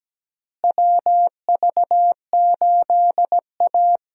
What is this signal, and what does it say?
Morse code, heard as a keyed tone, about 17 wpm, 700 Hz. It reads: WV8A
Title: USA Call Signs - 4 Characters - Repeated 3x - 17wpm